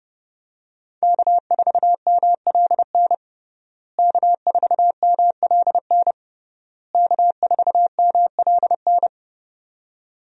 Morse code, heard as a keyed tone, about 32 wpm, 700 Hz. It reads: K4MLN K4MLN K4MLN
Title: USA Call Signs - 5 Characters - Repeated 3x - 30wpm